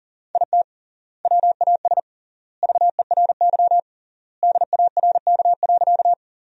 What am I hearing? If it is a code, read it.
Morse code, heard as a keyed tone, about 40 wpm, 700 Hz. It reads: IT WAS VERY DARK.
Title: 3x Word-Spacing - Sentences from Top 500 Words - 40wpm